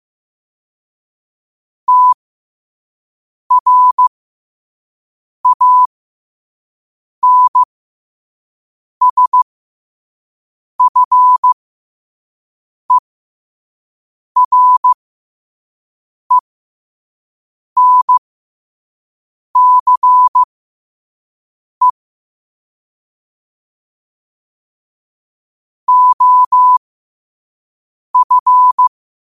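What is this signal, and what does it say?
Morse code, heard as a keyed tone, about 15 wpm, spaced as for 5 wpm, 1000 Hz. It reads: TRANSFERENCE OF
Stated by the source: Morse code practice at 5 wpm.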